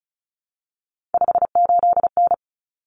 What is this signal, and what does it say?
Morse code, heard as a keyed tone, about 34 wpm, 700 Hz. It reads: 58N
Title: Common Cut Numbers - 3-Digit 35wpm